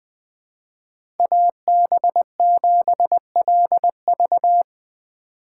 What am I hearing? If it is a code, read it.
Morse code, heard as a keyed tone, about 20 wpm, 700 Hz. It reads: AB7LV
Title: USA Call Signs - 4x Character Spacing - 20wpm